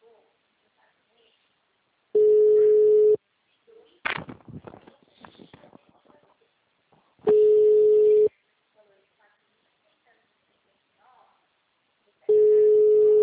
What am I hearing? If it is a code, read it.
Ringback tone (Europe, 425 Hz), with speech.